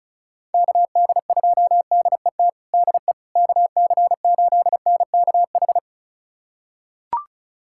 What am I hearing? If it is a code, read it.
Morse code, heard as a keyed tone, about 36 wpm, 700 Hz. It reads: KD2DET DE KC8NKH
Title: USA Callsign DE Callsign - 6 Characters - 35wpm